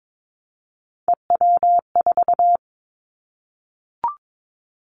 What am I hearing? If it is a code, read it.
Morse code, heard as a keyed tone, about 22 wpm, 700 Hz. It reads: EW4
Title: Callsign Prefixes for Top 100 DXCC - 22wpm